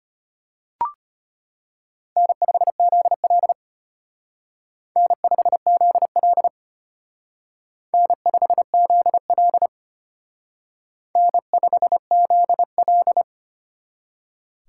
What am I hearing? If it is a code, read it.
Morse code, heard as a keyed tone, about 32 wpm, 700 Hz. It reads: N5ZL N5ZL N5ZL N5ZL